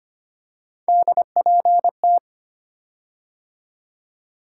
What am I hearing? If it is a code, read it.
Morse code, heard as a keyed tone, about 25 wpm, 700 Hz. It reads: DPT
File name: Three Letter-Number Combinations - 25wpm